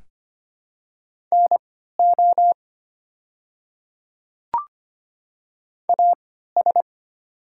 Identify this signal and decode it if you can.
Morse code, heard as a keyed tone, about 25 wpm, spaced as for 14 wpm, 700 Hz. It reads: NO AS
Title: 3x Character Spacing - Top 100 Words - 25wpm